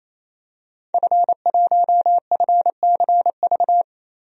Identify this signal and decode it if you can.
Morse code, heard as a keyed tone, about 28 wpm, 700 Hz. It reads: F1FCV
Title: DXCC-100 Call Signs - 5 Characters - 28wpm